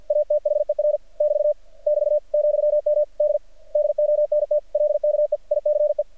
Morse code, heard as a keyed tone, about 50 wpm, 600 Hz. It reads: ATHER <BT> <BT> ,M D DONT LKE ICE
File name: archived / CRSnet